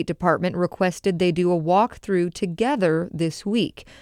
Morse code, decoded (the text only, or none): none